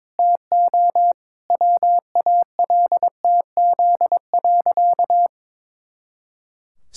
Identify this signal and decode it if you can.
Morse code, heard as a keyed tone, about 22 wpm, 700 Hz. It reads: TO WALTZ.
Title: Pangram Challenge v2 - 22wpm